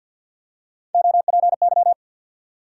Morse code, heard as a keyed tone, about 52 wpm, 700 Hz. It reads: OPY
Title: USA Call Sign Postfixes - 3 Letters 50wpm